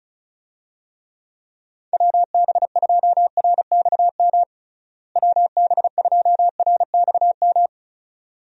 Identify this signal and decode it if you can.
Morse code, heard as a keyed tone, about 34 wpm, 700 Hz. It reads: WB2RXM WB2RXM
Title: USA Call Signs - 6 Characters - Repeated 2x - 35wpm